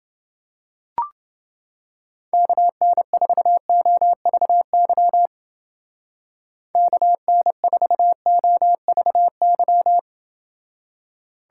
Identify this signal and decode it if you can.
Morse code, heard as a keyed tone, about 30 wpm, 700 Hz. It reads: KN4OVY KN4OVY